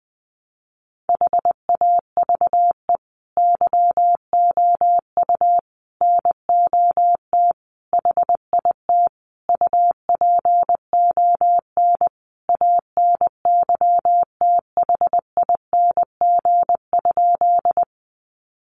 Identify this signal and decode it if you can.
Morse code, heard as a keyed tone, about 20 wpm, 700 Hz. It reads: HAVE YOU NOT HIT UPON ANYTHING?
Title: Sentences from Top 1200 Words 20wpm